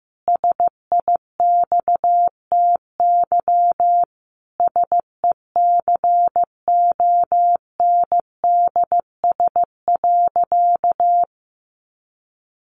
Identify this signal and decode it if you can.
Morse code, heard as a keyed tone, about 15 wpm, 700 Hz. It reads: SIXTY SECONDS.
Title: A Sound Of Thunder 15wpm